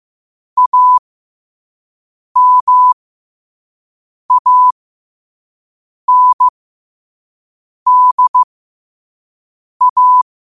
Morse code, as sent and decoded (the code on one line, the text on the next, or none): .- -- .- -. -.. .-
AMANDA